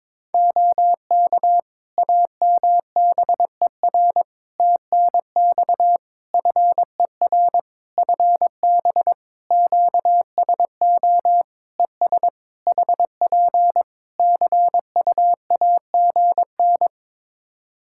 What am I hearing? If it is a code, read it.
Morse code, heard as a keyed tone, about 22 wpm, 700 Hz. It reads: OK AMBER TNX FER FB QSO ES HP CUAGN
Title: Practice QSO - Exchange 4 - Ending - 22wpm